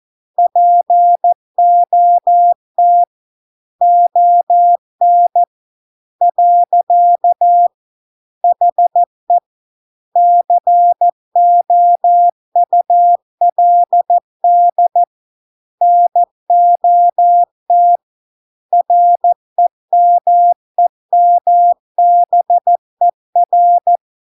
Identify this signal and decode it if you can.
Morse code, heard as a keyed tone, about 14 wpm, 700 Hz. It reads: POT ON . HE COULD NOT REMEMBER